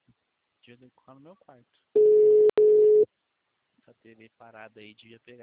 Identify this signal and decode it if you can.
Ringback tone (Europe, 425 Hz), with speech.